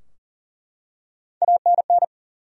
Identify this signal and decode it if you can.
Morse code, heard as a keyed tone, about 40 wpm, 700 Hz. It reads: ANN
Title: USA Common Names - Max 4 Letters - 40wpm